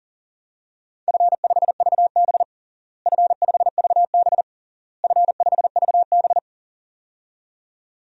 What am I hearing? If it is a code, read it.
Morse code, heard as a keyed tone, about 40 wpm, 700 Hz. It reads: F5VB F5VB F5VB
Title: DXCC-100 Call Signs - 4 Characters - Repeated 3x - 40wpm